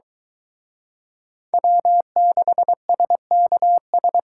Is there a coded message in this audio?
Yes: Morse code from 1.5 to 4.2 s, heard as a keyed tone, about 23 wpm, 700 Hz. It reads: W6SKS